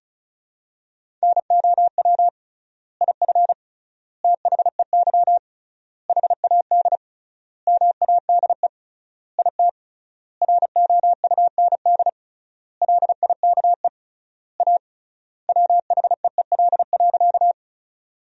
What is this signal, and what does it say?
Morse code, heard as a keyed tone, about 34 wpm, 700 Hz. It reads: NOW IF THEY HAD MADE IT ROUND LIKE A WHEEL.
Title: 3x Word-Spacing - Sentences from Top 500 Words - 35wpm